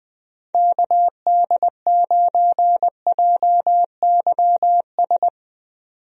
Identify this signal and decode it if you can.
Morse code, heard as a keyed tone, about 20 wpm, 700 Hz. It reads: KD9JYS
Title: USA Call Signs - 6 Characters - 20wpm